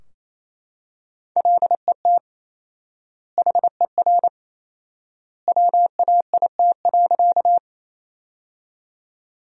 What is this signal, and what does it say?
Morse code, heard as a keyed tone, about 28 wpm, 700 Hz. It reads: LET HER WAIT.
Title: 4x Word-Spacing - Sentences from Top 500 Words - 28wpm